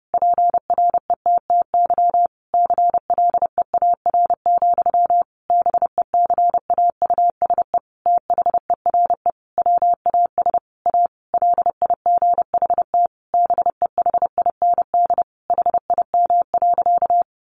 Morse code, heard as a keyed tone, about 30 wpm, 700 Hz. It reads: PRETTY CLEAR, BECAUSE THERE WAS A LIGHT BEHIND HIM.
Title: Sentences from Top 1000 Words 30wpm